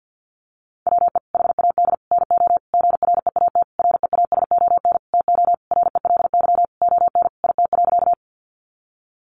Morse code, heard as a keyed tone, about 50 wpm, 700 Hz. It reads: WE HAD NO GREAT REASON TO RELY ON IT.
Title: Sentences from Top 1200 Words 50wpm